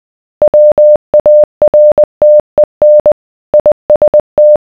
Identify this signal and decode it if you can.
Morse code, heard as a keyed tone, about 20 wpm, 600 Hz. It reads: WARTEN IST